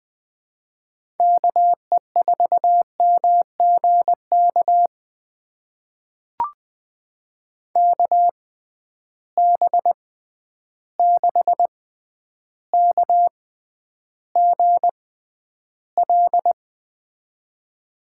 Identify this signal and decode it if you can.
Morse code, heard as a keyed tone, about 21 wpm, spaced as for 10 wpm, 700 Hz. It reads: KE4MGK KB6KGL
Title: USA Call Signs - 6 Characters - 6x Character Spacing - 20wpm